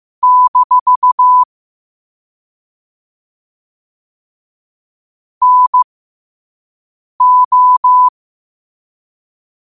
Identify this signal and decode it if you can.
Morse code, heard as a keyed tone, about 15 wpm, spaced as for 5 wpm, 1000 Hz. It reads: - NO
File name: qotdfiles / 5wpm